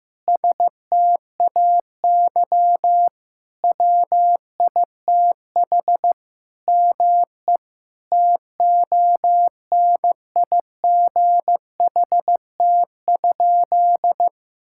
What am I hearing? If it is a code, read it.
Morse code, heard as a keyed tone, about 15 wpm, 700 Hz. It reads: STAY WITH ME TONIGHT?
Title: Sentences from Top 1600 Words 15wpm